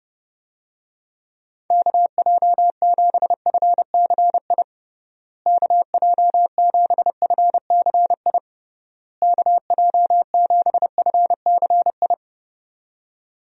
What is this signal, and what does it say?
Morse code, heard as a keyed tone, about 30 wpm, 700 Hz. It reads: KJ7FCI KJ7FCI KJ7FCI
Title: USA Call Signs - 6 Characters - Repeated 3x - 30wpm